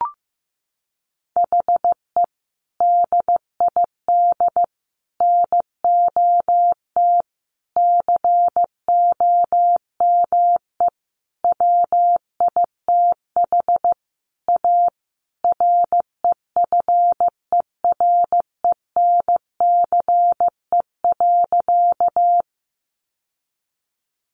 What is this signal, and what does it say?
Morse code, heard as a keyed tone, about 14 wpm, 700 Hz. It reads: HE DID NOT COME WITH A REFERENCE.
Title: Sentences from Top 1200 Words 15wpm